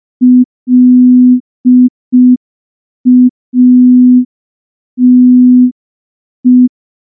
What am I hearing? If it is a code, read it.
Morse code, heard as a keyed tone, about 5 wpm, 250 Hz. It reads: LATE
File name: morse / morse-training-05WPM